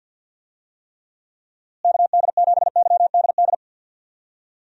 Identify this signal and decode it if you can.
Morse code, heard as a keyed tone, about 51 wpm, 700 Hz. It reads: KD7YDD
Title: USA Call Signs - 6 Characters - Repeated 2x - 50wpm